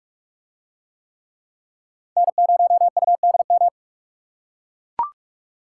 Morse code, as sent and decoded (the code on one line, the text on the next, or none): -. ----- ..- -.. --
N0UDM